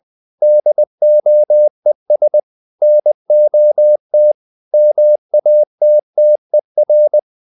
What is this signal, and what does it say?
Morse code, heard as a keyed tone, about 20 wpm, 600 Hz. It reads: DOES NOT MATTER
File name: the old man and the sea part 2 20wpm